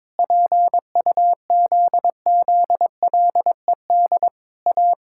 Morse code, heard as a keyed tone, about 22 wpm, 700 Hz. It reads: PUZZLED A
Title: Pangram Challenge v2 - 22wpm